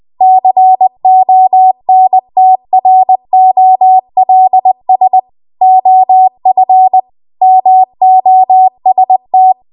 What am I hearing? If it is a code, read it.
Morse code, heard as a keyed tone, about 20 wpm, 750 Hz. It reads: CONTROLS OF MOST